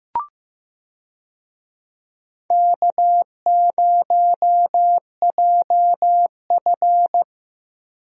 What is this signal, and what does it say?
Morse code, heard as a keyed tone, about 15 wpm, 700 Hz. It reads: K0JF
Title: USA Call Signs - 4 Characters - Repeated 2x - 15wpm